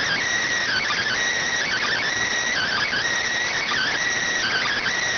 Emergency Alert System (SAME) tones (header bursts), heard as harsh buzzing bursts. It is heard from 0.6 to 5.0 s.